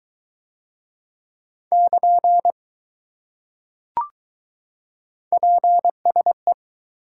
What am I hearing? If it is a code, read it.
Morse code, heard as a keyed tone, about 20 wpm, 700 Hz. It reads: <KN> PSE